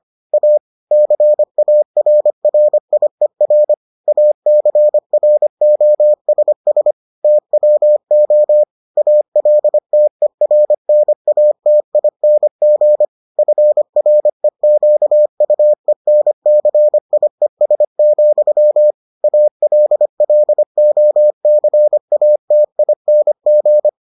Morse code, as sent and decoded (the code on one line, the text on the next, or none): .- / -.-. .- .-. .-. .. .-. / .- -.-. .-. --- ... ... / - .-- --- / .- .-.. - . .-. -. .- - .. -. --. / ..-. .-. --.- ..- . -. -.-. .. ... --..-- / .- .-.. .-.. --- -.-. .- - .. -. --.
A CARRIER ACROSS TWO ALTERNATING FREQUENCIES, ALLOCATING